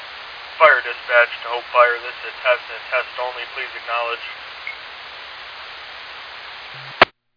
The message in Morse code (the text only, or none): none